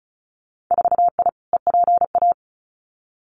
Morse code, heard as a keyed tone, about 34 wpm, 700 Hz. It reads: 4I EPA
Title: Winter Field Day Exchanges 35wpm